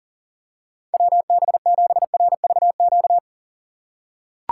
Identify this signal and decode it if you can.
Morse code, heard as a keyed tone, about 40 wpm, 700 Hz. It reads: WB7RVQ
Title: USA Call Signs - 6 Characters - 3x Character Spacing - 40wpm